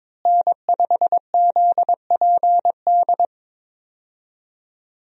Morse code, heard as a keyed tone, about 22 wpm, 700 Hz. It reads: N5ZPD
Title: USA Call Signs - 5 Characters - Repeated 2x - 22wpm